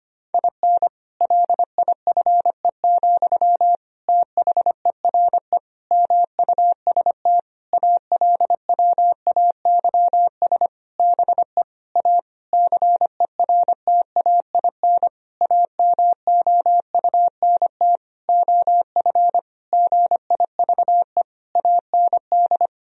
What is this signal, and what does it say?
Morse code, heard as a keyed tone, about 25 wpm, 700 Hz. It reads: IN LIFE, THERE MUST ALWAYS BE A CERTAIN AMOUNT OF GIVE AND
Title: Sentences from Top 1000 Words - Review Part 4 - 25wpm